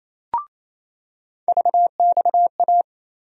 Morse code, heard as a keyed tone, about 28 wpm, 700 Hz. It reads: VXA
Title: USA Call Sign Postfixes - 3 Letters 28wpm